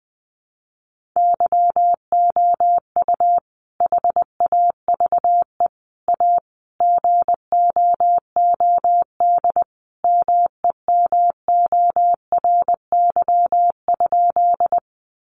Morse code, heard as a keyed tone, about 20 wpm, 700 Hz. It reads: YOU HAVE A GOOD MEMORY?
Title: Sentences from Top 1300 Words 20wpm